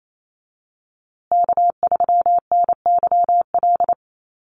Morse code, heard as a keyed tone, about 28 wpm, 700 Hz. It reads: K3NYL
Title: USA Call Signs Repeated 2x 28wpm